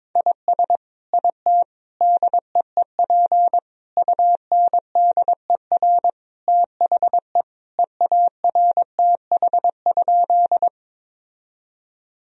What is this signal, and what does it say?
Morse code, heard as a keyed tone, about 22 wpm, 700 Hz. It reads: IS IT DEEP UNDER THE EARTH?
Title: Sentences from Top 1000 Words - Review Part 3 - 22wpm